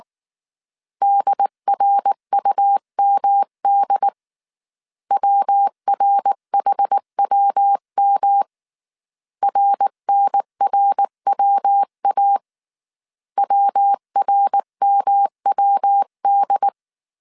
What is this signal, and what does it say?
Morse code, heard as a keyed tone, about 19 wpm, 800 Hz. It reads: DRUMD WRHWM RNRWA WRMWD